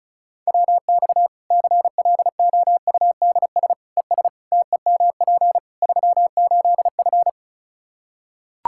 Morse code, heard as a keyed tone, about 34 wpm, 700 Hz. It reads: WX CLOUDS ES TEMP 38F
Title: Practice QSO - Exchange 2 - Info - 35wpm